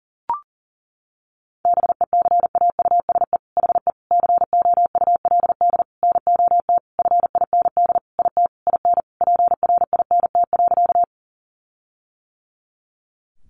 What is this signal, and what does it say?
Morse code, heard as a keyed tone, about 40 wpm, 700 Hz. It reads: BECAUSE HE COULD NOT FIND IT IN PRINT.
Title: Sentences from Top 1400 Words 40wpm